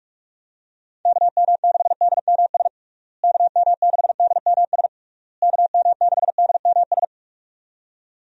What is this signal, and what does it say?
Morse code, heard as a keyed tone, about 41 wpm, 700 Hz. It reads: KM6DMS KM6DMS KM6DMS
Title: USA Call Signs - 6 Characters - Repeated 3x - 45wpm